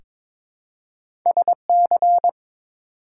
Morse code, heard as a keyed tone, about 21 wpm, 700 Hz. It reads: SC
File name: ARRL Sections v2 - 22wpm